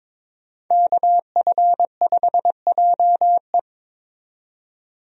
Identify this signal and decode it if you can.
Morse code, heard as a keyed tone, about 23 wpm, 700 Hz. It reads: KF5JE
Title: USA Call Signs - 5 Characters - 22wpm